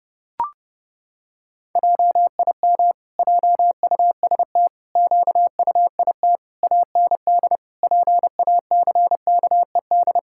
Morse code, heard as a keyed tone, about 32 wpm, 700 Hz. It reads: JIM JUST QUIT AND PACKED
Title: Pangram Challenge v2 - 30wpm